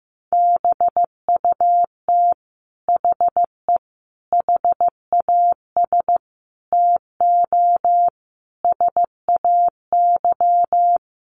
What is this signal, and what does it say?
Morse code, heard as a keyed tone, about 15 wpm, 700 Hz. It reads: BUT HE HAS TO SAY
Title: Sentences from Top 600 Words 15wpm